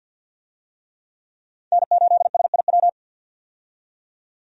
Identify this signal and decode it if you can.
Morse code, heard as a keyed tone, about 46 wpm, 700 Hz. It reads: N8SIW
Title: USA Call Signs - 5 Characters - 50wpm